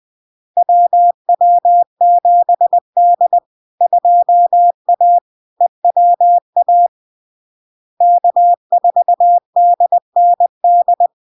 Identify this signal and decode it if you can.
Morse code, heard as a keyed tone, about 20 wpm, 700 Hz. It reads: WW7D 2A EWA K4DND
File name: LARK-FD-2022-Log-in-CW-20wpm-700-Hz